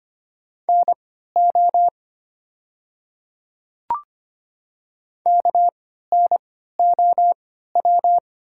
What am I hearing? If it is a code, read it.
Morse code, heard as a keyed tone, about 25 wpm, spaced as for 14 wpm, 700 Hz. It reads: NO KNOW
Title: 3x Character Spacing - Top 100 Words - 25wpm